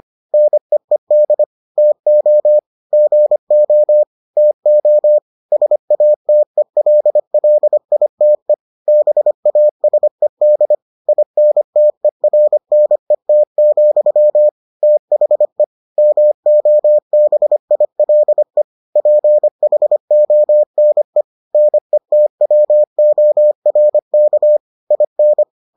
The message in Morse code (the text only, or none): NEED TO GO TO SATELLITE BASED INTERNET, THE MOBILE PHONE NETWORK IN